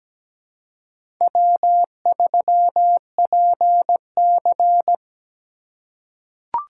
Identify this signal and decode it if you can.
Morse code, heard as a keyed tone, about 17 wpm, 700 Hz. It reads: W3PC
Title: USA Call Signs - 2x Character Spacing - 17wpm